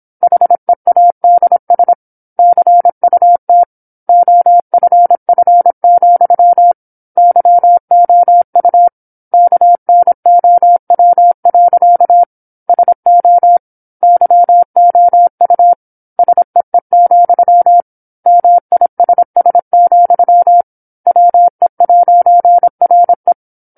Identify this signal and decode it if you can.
Morse code, heard as a keyed tone, about 27 wpm, 700 Hz. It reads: HEADS CUT OFF, YOU KNOW. SO YOU SEE, MISS, WE'RE
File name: 08-Chapter-08 26 WPM